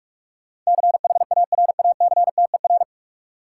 Morse code, heard as a keyed tone, about 41 wpm, 700 Hz. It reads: CHARACTER
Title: Sentences from Top 1200 Words - New Word Drill Alt - 45wpm